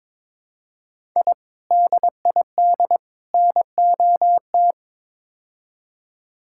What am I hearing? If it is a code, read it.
Morse code, heard as a keyed tone, about 21 wpm, 700 Hz. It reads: I DID NOT
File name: practice / 22wpm / Build A Phrase - 5-Word Phrases - 22wpm